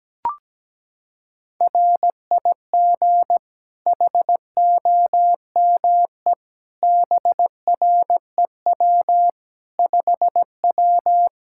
Morse code, heard as a keyed tone, about 17 wpm, 700 Hz. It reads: RIG HOME BREW 5W
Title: Practice QSO - Exchange 2 - Info - 17wpm